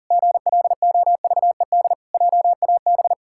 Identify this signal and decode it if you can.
Morse code, heard as a keyed tone, about 37 wpm, 700 Hz. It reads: GLOVED JAB
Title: Pangram Challenge v2 - 40wpm